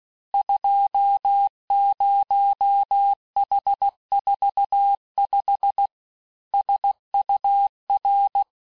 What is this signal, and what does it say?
Morse code, heard as a keyed tone, about 16 wpm, 800 Hz. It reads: 20H45 SUR
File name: cours-du-lundi files / Cours-1586-texte-chiffre-vitesse-15mpm